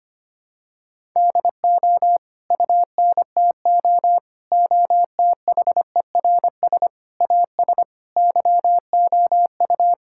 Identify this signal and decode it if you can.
Morse code, heard as a keyed tone, about 25 wpm, 700 Hz. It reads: DO UNTO OTHERS AS YOU
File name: Sets of 9 Words - 25wpm